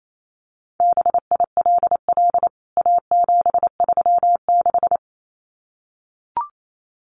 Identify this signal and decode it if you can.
Morse code, heard as a keyed tone, about 28 wpm, 700 Hz. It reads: BILL A736
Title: CWT Exchange 28wpm